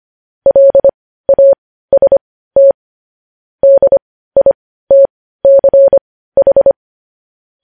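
Morse code, heard as a keyed tone, about 25 wpm, spaced as for 15 wpm, 550 Hz. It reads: LAST DITCH
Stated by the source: Morse code news headlines sent at 15.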